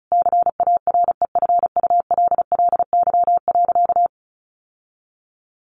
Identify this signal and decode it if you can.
Morse code, heard as a keyed tone, about 34 wpm, 700 Hz. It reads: CAREFULLY.
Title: Sentences from Top 1300 Words 35wpm